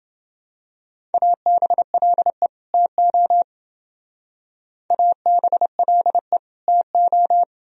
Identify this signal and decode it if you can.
Morse code, heard as a keyed tone, about 28 wpm, 700 Hz. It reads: ABLE TO ABLE TO